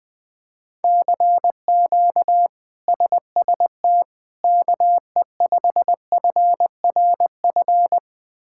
Morse code, heard as a keyed tone, about 20 wpm, 700 Hz. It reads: CQ SST KE5FRF
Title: K1USN SST QSO Practice - 20wpm